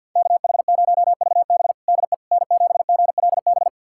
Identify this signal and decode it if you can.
Morse code, heard as a keyed tone, about 51 wpm, 700 Hz. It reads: KH0VB DE N7GLB
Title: USA Callsign DE Callsign - 5 Characters - 50wpm